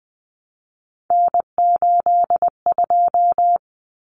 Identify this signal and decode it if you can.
Morse code, heard as a keyed tone, about 20 wpm, 700 Hz. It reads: N82
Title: Common Cut Numbers - 3-Digit 20wpm